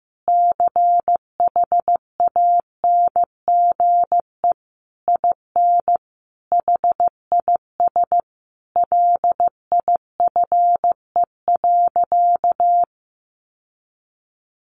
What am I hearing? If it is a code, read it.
Morse code, heard as a keyed tone, about 15 wpm, 700 Hz. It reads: CHANGE IN HIS LIFE.
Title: Sentences from Top 1100 Words 15wpm